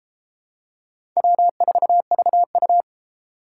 Morse code, heard as a keyed tone, about 30 wpm, 700 Hz. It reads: W4VU